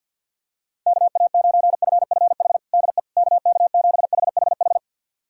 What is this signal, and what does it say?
Morse code, heard as a keyed tone, about 47 wpm, 700 Hz. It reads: KA9FFH DE KK7HHH
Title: USA Callsign DE Callsign - 6 Characters - 50wpm